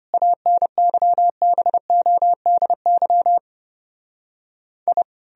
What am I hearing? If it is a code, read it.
Morse code, heard as a keyed tone, about 33 wpm, 700 Hz. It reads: ANYBODY I